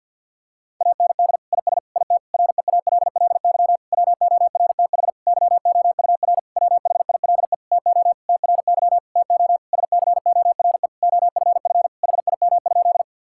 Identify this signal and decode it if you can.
Morse code, heard as a keyed tone, about 50 wpm, 700 Hz. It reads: AND IS IT REALLY WORTH YOUR WHILE TO TRY TO SCORE OFF HIM?
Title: Sentences from Top 1600 Words 50wpm